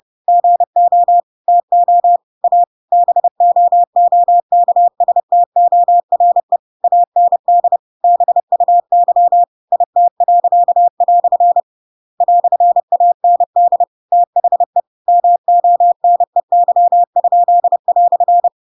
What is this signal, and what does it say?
Morse code, heard as a keyed tone, about 30 wpm, 700 Hz. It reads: GO TO A BOOKSTORE AND BUY IT." "AND THE MONEY?"
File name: Pinocchio / Pinocchio ch8 30 30 ik1hge